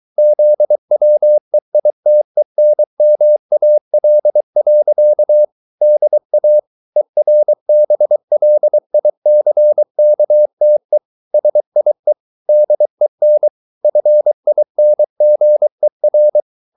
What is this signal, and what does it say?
Morse code, heard as a keyed tone, about 23 wpm, 600 Hz. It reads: ZWEITENMAL. DA ERBLICKTE SIE DEN FINGER